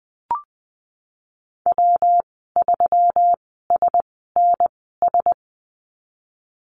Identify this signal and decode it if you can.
Morse code, heard as a keyed tone, about 20 wpm, spaced as for 15 wpm, 700 Hz. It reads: W3SNS
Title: 173 Lesson 28g - Number 9 - Call Signs 20wpm